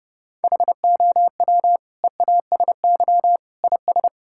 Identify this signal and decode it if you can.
Morse code, heard as a keyed tone, about 32 wpm, 700 Hz. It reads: HOW EASY IS
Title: Sentences from Top 1000 Words - Review Part 6 - 30wpm